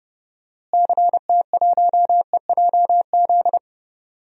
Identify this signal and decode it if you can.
Morse code, heard as a keyed tone, about 31 wpm, 700 Hz. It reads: CT1EJZ
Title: DXCC-100 Call Signs - 6 Characters - Repeated 2x - 30wpm